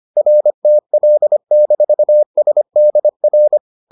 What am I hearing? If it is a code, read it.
Morse code, heard as a keyed tone, about 25 wpm, 600 Hz. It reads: RTL-SDR